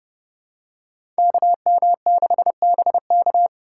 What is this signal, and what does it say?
Morse code, heard as a keyed tone, about 30 wpm, 700 Hz. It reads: KM6BK
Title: USA Call Signs - 5 Characters - 3x Character Spacing - 30wpm